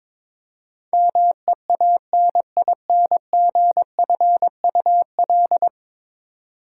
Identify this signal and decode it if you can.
Morse code, heard as a keyed tone, about 22 wpm, 700 Hz. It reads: MEANINGFUL
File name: practice / 22wpm / 10 Letter Words - First 3 Letters Given - 22wpm